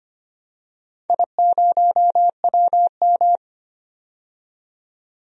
Morse code, heard as a keyed tone, about 25 wpm, 700 Hz. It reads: I0WM
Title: DXCC-100 Call Signs - 4 Characters - 25wpm